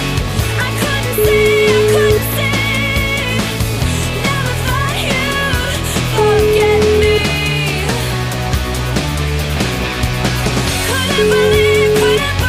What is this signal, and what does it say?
Ringback tone (Europe, 425 Hz), with noise.